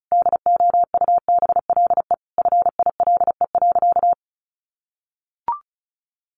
Morse code, heard as a keyed tone, about 34 wpm, 700 Hz. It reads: DOUBLE FILE.
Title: Sentences from Top 1400 Words 35wpm